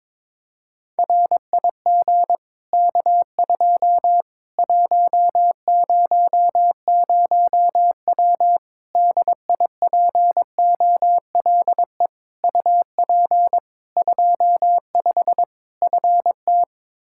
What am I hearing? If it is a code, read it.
Morse code, heard as a keyed tone, about 22 wpm, 700 Hz. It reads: RIG K2 100W DIPOLE UP 25 FT
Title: Practice QSO - Exchange 2 - Info - 22wpm